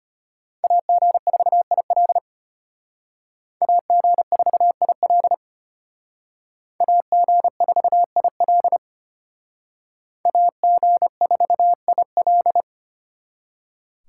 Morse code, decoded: AG4IL AG4IL AG4IL AG4IL

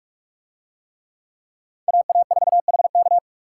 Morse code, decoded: AA4HK

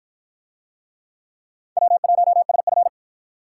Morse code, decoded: W1SF

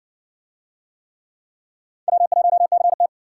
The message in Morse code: W1CT

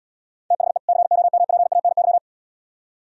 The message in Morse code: THE COMPANY

